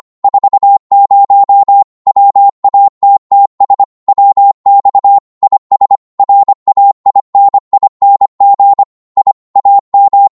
Morse code, heard as a keyed tone, about 25 wpm, 800 Hz. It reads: 40 WATTS WX IS RAINING I AM
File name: Morse / qso-25wpm